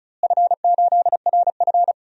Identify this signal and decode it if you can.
Morse code, heard as a keyed tone, about 36 wpm, 700 Hz. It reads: F8RF
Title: DXCC-100 Call Signs - 4 Characters - Repeated 3x - 35wpm